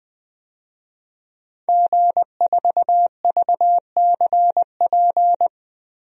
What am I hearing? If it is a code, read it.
Morse code, heard as a keyed tone, about 20 wpm, 700 Hz. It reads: G4VCP
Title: DXCC-100 Call Signs - 5 Characters - Repeated 2x - 20wpm